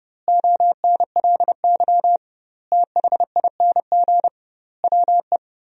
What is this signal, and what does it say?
Morse code, heard as a keyed tone, about 30 wpm, 700 Hz. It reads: ONLY THING WE
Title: 2x Word-Spacing - Sentences from Top 200 Words - 30wpm